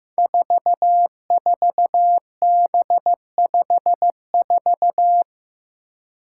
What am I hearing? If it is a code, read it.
Morse code, heard as a keyed tone, about 15 wpm, 700 Hz. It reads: 44B54
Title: Five Challenging Letter-Number Combinations 15wpm